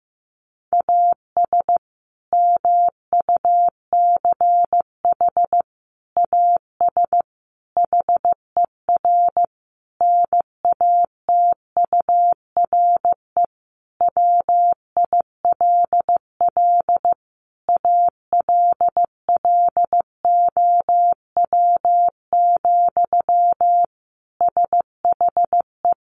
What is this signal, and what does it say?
Morse code, heard as a keyed tone, about 15 wpm, 700 Hz. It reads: AS MUCH AS HER NATURE WILL ALLOW, SHE
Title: Sentences from Top 700 Words 15wpm